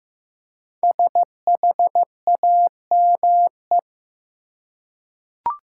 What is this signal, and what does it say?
Morse code, heard as a keyed tone, about 15 wpm, 700 Hz. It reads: SHAME
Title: Five Letter Words - 15wpm